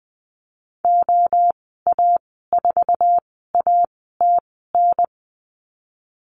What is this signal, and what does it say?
Morse code, heard as a keyed tone, about 20 wpm, spaced as for 15 wpm, 700 Hz. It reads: OA4ATN
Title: 055 Lesson 10g - Letter r - Call Signs 20wpm